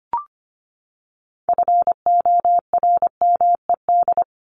Morse code, heard as a keyed tone, about 25 wpm, 700 Hz. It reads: FORMED
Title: ED Suffix - Top 100 Words with Base Words 25wpm